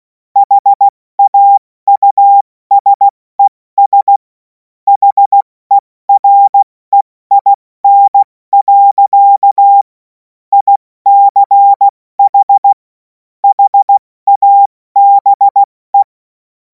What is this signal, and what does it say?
Morse code, heard as a keyed tone, about 16 wpm, 800 Hz. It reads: HAUSES HEREIN. ICH HABE
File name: Morsekurs / Witze / Witz 10. März 2026 70 F80